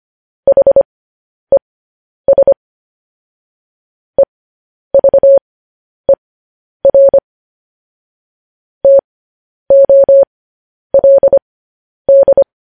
Morse code, heard as a keyed tone, about 25 wpm, spaced as for 10 wpm, 550 Hz. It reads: HES EVER TOLD
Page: News Headlines in Morse Code at 10 WPM